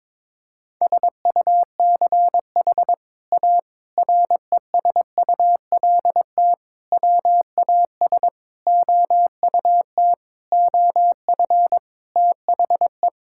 Morse code, heard as a keyed tone, about 22 wpm, 700 Hz. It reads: SUCH A RESULT WAS OUT OF THE
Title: Sentences from Top 1000 Words - Review Part 5 - 22wpm